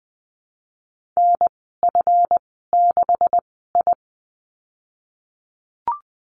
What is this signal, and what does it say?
Morse code, heard as a keyed tone, about 20 wpm, spaced as for 14 wpm, 700 Hz. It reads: NF6I
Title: 161 Lesson 26g - Letter g - Call Signs 20wpm